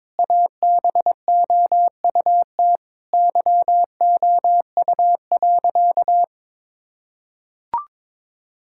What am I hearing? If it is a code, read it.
Morse code, heard as a keyed tone, about 22 wpm, 700 Hz. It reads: ABOUT YOU.
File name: Sentences with 7-Word Phrases - 22wpm